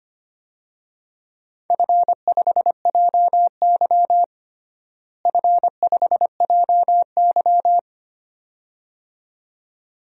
Morse code, heard as a keyed tone, about 25 wpm, 700 Hz. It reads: F5JY F5JY
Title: DXCC-100 Call Signs - 4 Characters - Repeated 2x - 25wpm